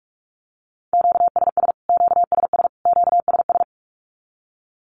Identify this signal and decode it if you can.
Morse code, heard as a keyed tone, about 44 wpm, 700 Hz. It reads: QSS QSS QSS
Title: Q-Codes Repeated 3x 45wpm